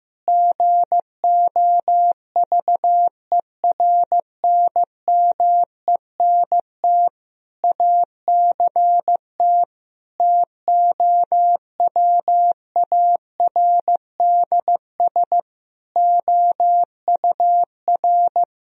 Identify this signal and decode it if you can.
Morse code, heard as a keyed tone, about 15 wpm, 700 Hz. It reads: GOVERNMENT ACT TOWARDS OUR
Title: Sentences from Top 1000 Words - Review Part 4 - 15wpm